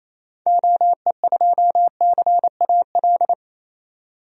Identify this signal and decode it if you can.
Morse code, heard as a keyed tone, about 28 wpm, 700 Hz. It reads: OE2CAL
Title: DXCC-100 Call Signs - 6 Characters - 28wpm